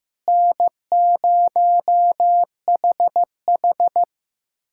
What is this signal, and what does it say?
Morse code, heard as a keyed tone, about 16 wpm, 700 Hz. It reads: N0HH
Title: USA Callsign DE Callsign - 4 Characters - 15wpm